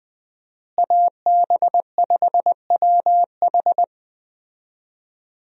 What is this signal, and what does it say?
Morse code, heard as a keyed tone, about 21 wpm, 700 Hz. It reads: AB5WH
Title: USA Call Signs - 5 Characters - Repeated 2x - 20wpm